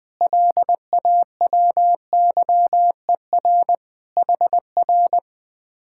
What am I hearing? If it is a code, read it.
Morse code, heard as a keyed tone, about 21 wpm, 700 Hz. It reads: LAWYER HR
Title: Speed-Racing Common QSO Phrases - 20wpm